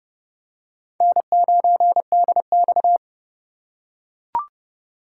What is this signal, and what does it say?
Morse code, heard as a keyed tone, about 29 wpm, 700 Hz. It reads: N9DX